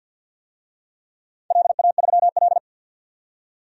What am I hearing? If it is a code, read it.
Morse code, heard as a keyed tone, about 46 wpm, 700 Hz. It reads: LA3L
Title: DXCC-100 Call Signs - 4 Characters - 50wpm